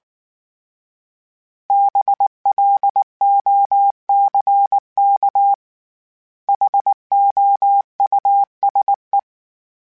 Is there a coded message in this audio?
Yes: Morse code from 1.7 to 9.2 s, heard as a keyed tone, about 19 wpm, 800 Hz. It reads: BLOCK HOUSE